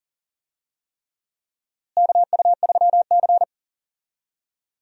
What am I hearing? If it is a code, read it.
Morse code, heard as a keyed tone, about 39 wpm, 700 Hz. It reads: KU3C
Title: USA Call Signs - 4 Characters - Repeated 3x - 40wpm